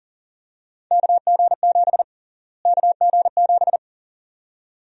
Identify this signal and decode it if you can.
Morse code, heard as a keyed tone, about 40 wpm, 700 Hz. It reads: KG7 KG7
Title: USA Call Sign Prefixes - Repeated 2x - 40wpm